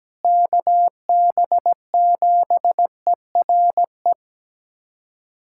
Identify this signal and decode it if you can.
Morse code, heard as a keyed tone, about 17 wpm, 700 Hz. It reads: KB7ERE
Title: USA Callsign DE Callsign - 6 Characters - 17wpm